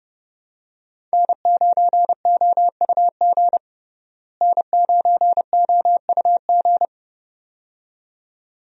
Morse code, heard as a keyed tone, about 32 wpm, 700 Hz. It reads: N9OUG N9OUG
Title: USA Call Signs - 5 Characters - Repeated 2x - 30wpm